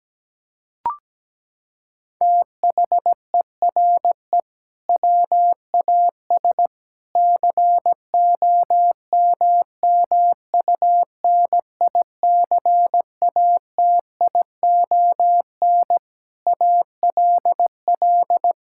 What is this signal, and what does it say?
Morse code, heard as a keyed tone, about 17 wpm, 700 Hz. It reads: THERE WAS COMMUNICATION ALL
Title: Sentences from Top 1000 Words - Review Part 6 - 17wpm